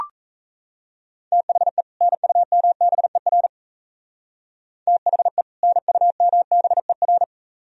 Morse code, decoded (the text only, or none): THE NUMBER THE NUMBER